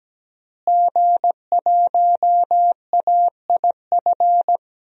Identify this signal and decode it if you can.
Morse code, heard as a keyed tone, about 17 wpm, 700 Hz. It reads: G1AIF
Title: DXCC-100 Call Signs - 5 Characters - Repeated 3x - 17wpm